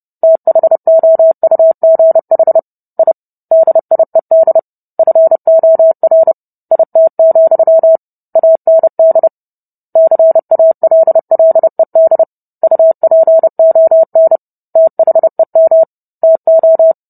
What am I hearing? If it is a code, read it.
Morse code, heard as a keyed tone, about 30 wpm, 650 Hz. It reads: THOUGH I DIED FOR IT, AND CALLED UPON THEM TO